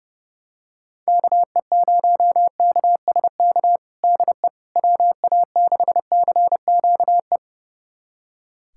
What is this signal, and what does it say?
Morse code, heard as a keyed tone, about 32 wpm, 700 Hz. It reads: KE0KSK DE WA6CQE
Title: USA Callsign DE Callsign 30wpm